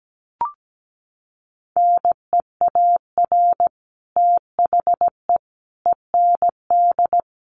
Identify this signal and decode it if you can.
Morse code, heard as a keyed tone, about 17 wpm, 700 Hz. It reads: NEAR THE END
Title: Sets of 5 Words Encore 17wpm